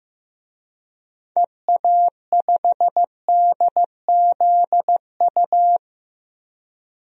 Morse code, heard as a keyed tone, about 15 wpm, 700 Hz. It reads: EA5DZU
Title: DXCC-100 Call Signs - 6 Characters - Repeated 3x - 15wpm